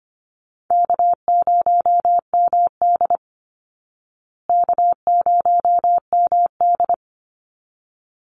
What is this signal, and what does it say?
Morse code, heard as a keyed tone, about 25 wpm, 700 Hz. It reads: K0MD K0MD
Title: Call Signs Repeated 3x - 25wpm